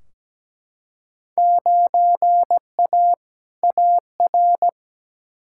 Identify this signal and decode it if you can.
Morse code, heard as a keyed tone, about 17 wpm, 700 Hz. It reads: 9A AR
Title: Field Day Exchanges Encore - 17wpm